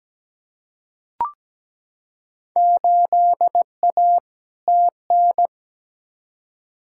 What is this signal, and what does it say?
Morse code, heard as a keyed tone, about 17 wpm, 700 Hz. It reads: 8A TN
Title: Field Day Exchanges Encore - 17wpm